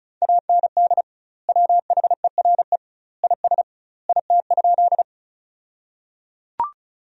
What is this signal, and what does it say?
Morse code, heard as a keyed tone, about 36 wpm, 700 Hz. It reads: AND WHERE IS IT?
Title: 2x Word-Spacing - Sentences from Top 200 Words - 35wpm